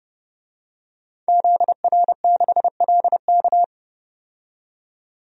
Morse code, heard as a keyed tone, about 30 wpm, 700 Hz. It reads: ZR6LK
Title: DXCC-100 Call Signs - 5 Characters - 30wpm